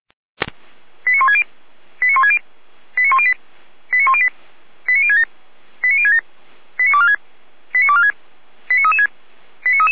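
Five-tone selective call (ZVEI), heard as quick runs of five tones, tones of about 70 ms. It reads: 89160 89160 89108 89108 89067 89067 89246 89246 89336
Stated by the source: Five-tone ZVEI selective call received on SDR, from the feuerwehr-orlamuende.org fire brigade audio server.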